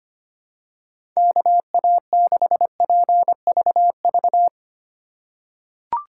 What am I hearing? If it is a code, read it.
Morse code, heard as a keyed tone, about 25 wpm, 700 Hz. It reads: KA6PVV
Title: USA Call Signs - 6 Characters - 2x Character Spacing - 25wpm